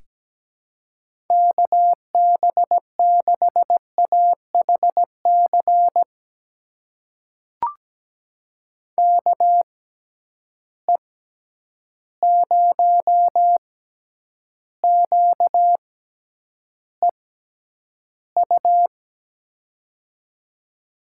Morse code, heard as a keyed tone, about 17 wpm, spaced as for 9 wpm, 700 Hz. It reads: KB6AHC KE0QEU
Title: USA Call Signs - 6 Characters - 6x Character Spacing - 17wpm